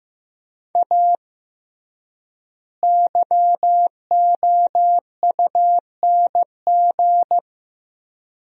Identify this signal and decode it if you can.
Morse code, heard as a keyed tone, about 15 wpm, 700 Hz. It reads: A YOUNG
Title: 3x Word-Spacing - Sentences from Top 400 Words - 15wpm